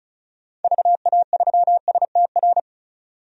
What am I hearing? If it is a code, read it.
Morse code, heard as a keyed tone, about 34 wpm, 700 Hz. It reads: VA3STR
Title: DXCC-100 Call Signs - 6 Characters - 35wpm